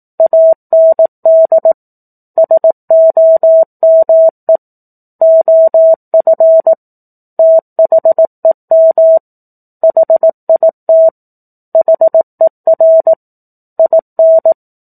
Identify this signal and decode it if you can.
Morse code, heard as a keyed tone, about 18 wpm, 650 Hz. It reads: AND SOME OF THEM HIT HER IN
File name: 04-Chapter-04 18 WPM